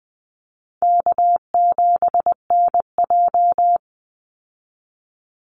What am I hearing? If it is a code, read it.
Morse code, heard as a keyed tone, about 20 wpm, 700 Hz. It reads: K7NJ
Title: Easy Call Signs Repeated 2x - 20wpm